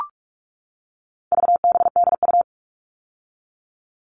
Morse code, heard as a keyed tone, about 44 wpm, 700 Hz. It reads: VBDU
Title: All Cut Numbers - 4-Digit 45wpm